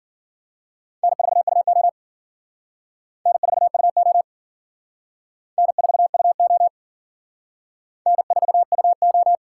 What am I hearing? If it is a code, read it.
Morse code, heard as a keyed tone, about 50 wpm, 700 Hz. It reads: N4UO N4UO N4UO N4UO
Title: Speed Racing - USA 1x2 Call Signs 40wpm